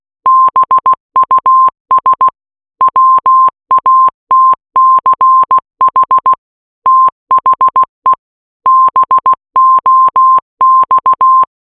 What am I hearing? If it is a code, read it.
Morse code, heard as a keyed tone, about 16 wpm, 1050 Hz. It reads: BUS WATCH THE BOX